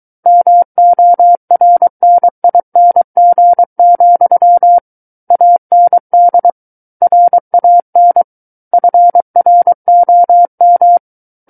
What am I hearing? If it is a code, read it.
Morse code, heard as a keyed tone, about 23 wpm, 700 Hz. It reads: MORNING, AND RAN FROM
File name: Book One / 14-Chapter 14 - 23 WPM